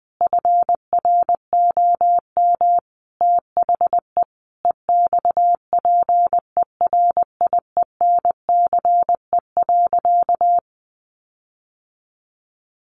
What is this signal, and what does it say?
Morse code, heard as a keyed tone, about 20 wpm, 700 Hz. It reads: FROM THE EXPERIENCE.
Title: Sentences from Top 1300 Words 20wpm